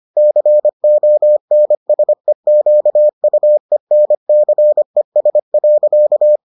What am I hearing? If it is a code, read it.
Morse code, heard as a keyed tone, about 25 wpm, 600 Hz. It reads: CONSEQUENCES.